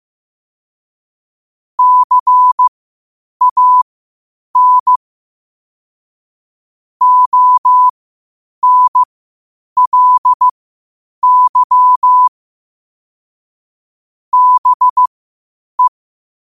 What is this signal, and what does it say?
Morse code, heard as a keyed tone, about 15 wpm, spaced as for 8 wpm, 1000 Hz. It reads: CAN ONLY BE
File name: qotdfiles / 8wpm